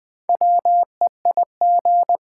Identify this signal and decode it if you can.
Morse code, heard as a keyed tone, about 20 wpm, 700 Hz. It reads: WEIG
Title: Build A Word - 2 Letters At A Time - 6-Letter Words - 20wpm